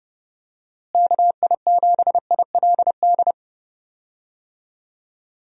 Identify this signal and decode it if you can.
Morse code, heard as a keyed tone, about 30 wpm, 700 Hz. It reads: KI7ILD
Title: USA Call Signs - 6 Characters - Repeated 3x - 30wpm